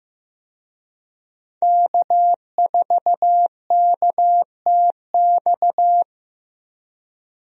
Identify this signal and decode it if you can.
Morse code, heard as a keyed tone, about 15 wpm, 700 Hz. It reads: K4KTX
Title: USA Call Signs - 5 Characters - 15wpm